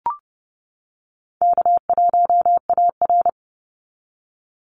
Morse code, heard as a keyed tone, about 30 wpm, 700 Hz. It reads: K1AR
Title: Call Signs - 30wpm